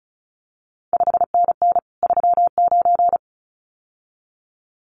Morse code, heard as a keyed tone, about 36 wpm, 700 Hz. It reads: 5NN 39
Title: CQ WW DX Contest Exchange - 35wpm